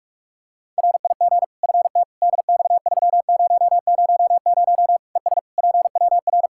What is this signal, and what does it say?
Morse code, heard as a keyed tone, about 46 wpm, 700 Hz. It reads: RIG FT DX3000 ES PWR
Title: Practice QSO - Exchange 2 - Info - 45wpm